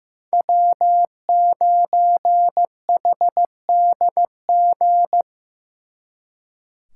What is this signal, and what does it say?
Morse code, heard as a keyed tone, about 18 wpm, 700 Hz. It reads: W9HDG
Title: Speed Racing - USA 1x3 Call Signs 15wpm